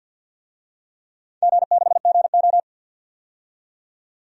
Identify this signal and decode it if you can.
Morse code, heard as a keyed tone, about 51 wpm, 700 Hz. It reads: G6GO
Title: DXCC-100 Call Signs - 4 Characters - 50wpm